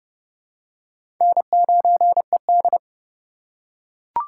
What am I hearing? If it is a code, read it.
Morse code, heard as a keyed tone, about 29 wpm, 700 Hz. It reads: N9ED
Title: USA Call Signs - 4 Characters - 3x Character Spacing - 30wpm